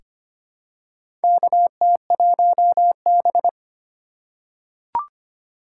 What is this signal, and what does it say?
Morse code, heard as a keyed tone, about 25 wpm, 700 Hz. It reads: KT1B
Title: K1USN SST QSO Practice - 25wpm